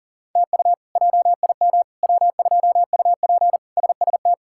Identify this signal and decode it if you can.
Morse code, heard as a keyed tone, about 40 wpm, 700 Hz. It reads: TU JIM W2UP SST
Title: K1USN SST QSO Practice - 40wpm